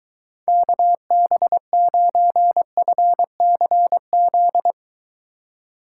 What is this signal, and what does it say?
Morse code, heard as a keyed tone, about 23 wpm, 700 Hz. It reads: KB9FCZ